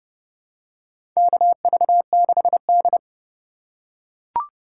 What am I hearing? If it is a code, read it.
Morse code, heard as a keyed tone, about 31 wpm, 700 Hz. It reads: KV6D
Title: USA Call Signs - 3x Character Spacing - 30wpm